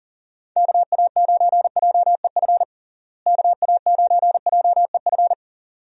Morse code, heard as a keyed tone, about 40 wpm, 700 Hz. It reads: KA9JEF KA9JEF
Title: USA Call Signs - 6 Characters - Repeated 2x - 40wpm